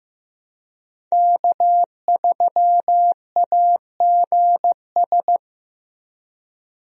Morse code, heard as a keyed tone, about 15 wpm, 700 Hz. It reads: K3AGS
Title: USA Call Signs - 5 Characters - Repeated 3x - 15wpm